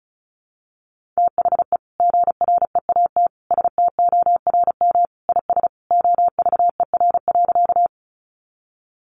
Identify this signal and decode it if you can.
Morse code, heard as a keyed tone, about 34 wpm, 700 Hz. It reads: THE GREAT STORM IS OVER.
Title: Sentences from Top 1500 Words 35wpm